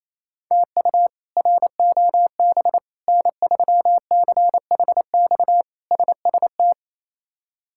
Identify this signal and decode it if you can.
Morse code, heard as a keyed tone, about 28 wpm, 700 Hz. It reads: TU ROB N3CHX SST
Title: K1USN SST QSO Practice - 28wpm